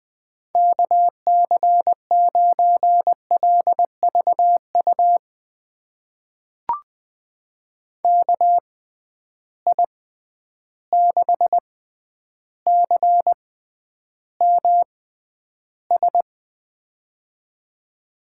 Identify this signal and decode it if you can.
Morse code, heard as a keyed tone, about 20 wpm, spaced as for 10 wpm, 700 Hz. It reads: KC9LVU KI6CMS
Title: USA Call Signs - 6 Characters - 6x Character Spacing - 20wpm